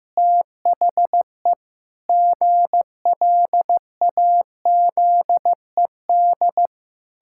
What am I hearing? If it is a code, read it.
Morse code, heard as a keyed tone, about 15 wpm, 700 Hz. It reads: THE GLAZED
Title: Pangram Challenge v2 - 15wpm